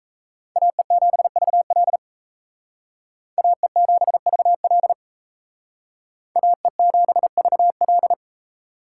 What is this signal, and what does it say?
Morse code, heard as a keyed tone, about 35 wpm, 700 Hz. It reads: AE7VL AE7VL AE7VL